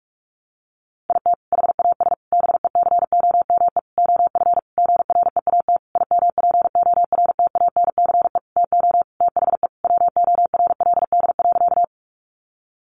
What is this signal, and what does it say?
Morse code, heard as a keyed tone, about 44 wpm, 700 Hz. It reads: IT HAS BECOME OF GREAT IMPORTANCE TO THE WORLD.
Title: Sentences from Top 1200 Words 45wpm